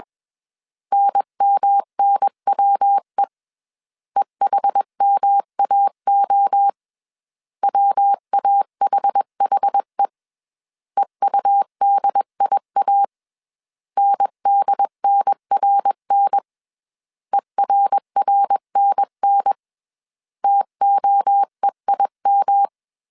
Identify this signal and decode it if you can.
Morse code, heard as a keyed tone, about 21 wpm, 800 Hz. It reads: NMNWE EHMAO WAHHE EUDIA NDNRN ERRNN TOEIM
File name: CW / DET / 1 3 1500C S B 20wpm 1754